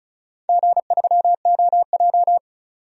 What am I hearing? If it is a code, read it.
Morse code, heard as a keyed tone, about 36 wpm, 700 Hz. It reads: G3OJ
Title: DXCC-100 Call Signs - 4 Characters - 35wpm